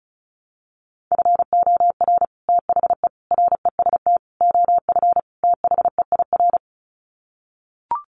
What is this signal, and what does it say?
Morse code, heard as a keyed tone, about 34 wpm, 700 Hz. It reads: FOR THE REST OF THEIR
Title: Sets of 5 Words Encore2 35wpm